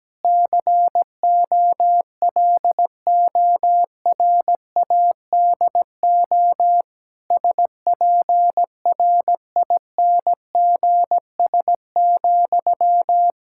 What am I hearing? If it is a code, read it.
Morse code, heard as a keyed tone, about 17 wpm, 700 Hz. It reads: COLORADO SPRINGS,
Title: Practice QSO - Exchange 1 - Beginning - 17wpm